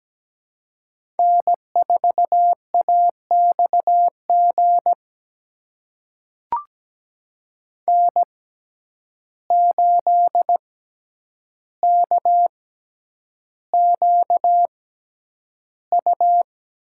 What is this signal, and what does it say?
Morse code, heard as a keyed tone, about 17 wpm, spaced as for 9 wpm, 700 Hz. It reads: N4AXG N8KQU
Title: USA Call Signs - 5 Characters - 6x Character Spacing - 17wpm